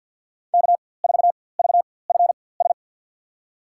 Morse code, heard as a keyed tone, about 47 wpm, spaced as for 23 wpm, 700 Hz. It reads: K4VFS
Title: USA Call Signs - 5 Characters - 4x Character Spacing - 50wpm